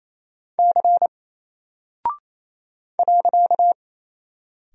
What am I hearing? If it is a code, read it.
Morse code, heard as a keyed tone, about 28 wpm, 700 Hz. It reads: C .
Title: Single Character (Letter:Number:Punctuation:Prosign) Alt 28wpm